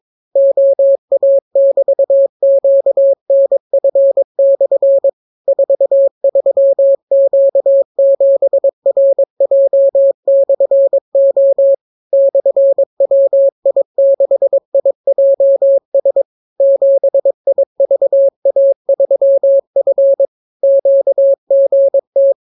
Morse code, heard as a keyed tone, about 22 wpm, 550 Hz. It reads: OA<BT>QNF/ 43Q7RJ/O /WI6IJS 7IVA3F QGT